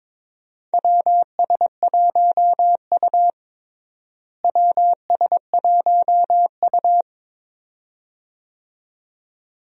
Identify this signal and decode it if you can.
Morse code, heard as a keyed tone, about 22 wpm, 700 Hz. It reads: WS1U WS1U
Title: USA Call Signs - 4 Characters - Repeated 2x - 22wpm